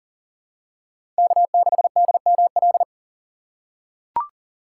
Morse code, heard as a keyed tone, about 37 wpm, 700 Hz. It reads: K6DML
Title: USA Call Signs - 5 Characters - 8x Character Spacing - 40wpm